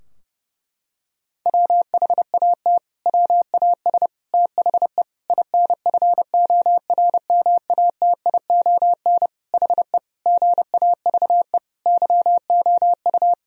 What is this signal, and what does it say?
Morse code, heard as a keyed tone, about 30 wpm, 700 Hz. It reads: WHAT WAS THE INFORMATION HE GAVE YOU
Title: Sentences from Top 1000 Words - Review Part 3 - 30wpm